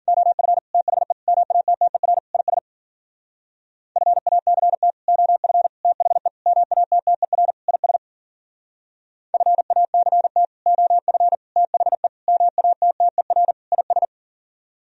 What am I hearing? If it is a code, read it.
Morse code, heard as a keyed tone, about 56 wpm, 700 Hz. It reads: OF THE MATTER IS FACT OF THE MATTER IS FACT OF THE MATTER IS